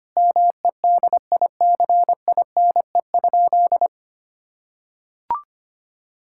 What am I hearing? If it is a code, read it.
Morse code, heard as a keyed tone, about 25 wpm, 700 Hz. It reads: MEDICINE?
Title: Sentences from Top 1000 Words - Review Part 6 - 25wpm